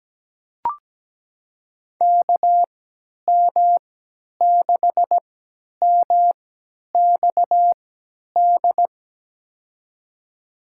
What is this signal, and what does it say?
Morse code, heard as a keyed tone, about 17 wpm, spaced as for 10 wpm, 700 Hz. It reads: KM6MXD
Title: USA Call Signs - 6 Characters - 3x Character Spacing - 17wpm